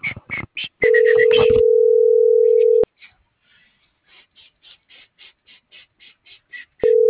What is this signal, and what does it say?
Ringback tone (North America), 440 + 480 Hz.